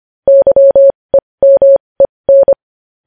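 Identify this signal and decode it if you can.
Morse code, heard as a keyed tone, about 25 wpm, spaced as for 20 wpm, 550 Hz. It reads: YEMEN
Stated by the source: Morse practice sending news headlines at 20 wpm.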